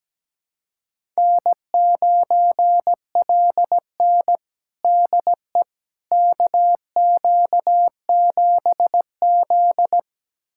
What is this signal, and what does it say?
Morse code, heard as a keyed tone, about 17 wpm, 700 Hz. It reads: N9LN DE KQ7Z
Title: USA Callsign DE Callsign - 4 Characters - 17wpm